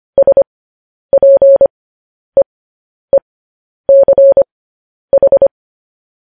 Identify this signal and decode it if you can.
Morse code, heard as a keyed tone, about 25 wpm, spaced as for 10 wpm, 550 Hz. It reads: SPEECH
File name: podcast / News headlines-1x-25x10